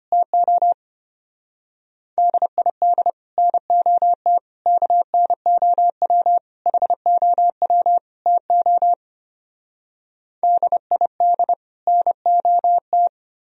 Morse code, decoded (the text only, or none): TO DID NOT KNOW HOW TO DID NOT